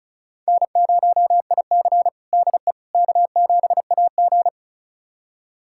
Morse code, heard as a keyed tone, about 36 wpm, 700 Hz. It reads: N0IC DE K7AG
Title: USA Callsign DE Callsign - 4 Characters - 35wpm